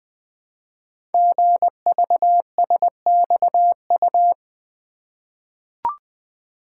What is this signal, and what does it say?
Morse code, heard as a keyed tone, about 20 wpm, 700 Hz. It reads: GVSXU
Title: Code Groups - 5 Letters - 20wpm